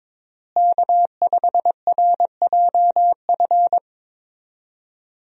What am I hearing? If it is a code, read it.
Morse code, heard as a keyed tone, about 22 wpm, 700 Hz. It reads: K5RJF
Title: USA Call Signs - 5 Characters - Repeated 2x - 22wpm